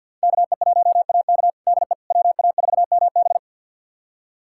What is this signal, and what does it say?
Morse code, heard as a keyed tone, about 49 wpm, 700 Hz. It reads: KE1AK DE WA4MB
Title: USA Callsign DE Callsign - 5 Characters - 50wpm